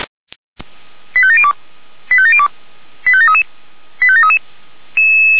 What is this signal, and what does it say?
Five-tone selective call (ZVEI), heard as quick runs of five tones, tones of about 70 ms. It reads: 85702 85702 85733 85733